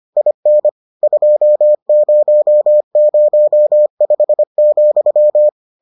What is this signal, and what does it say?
Morse code, heard as a keyed tone, about 25 wpm, 600 Hz. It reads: IN 2005,